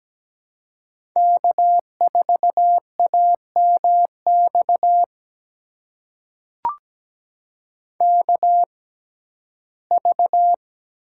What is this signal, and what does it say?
Morse code, heard as a keyed tone, about 17 wpm, spaced as for 12 wpm, 700 Hz. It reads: K4AMX KV
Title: USA Call Signs - 6x Character Spacing - 17wpm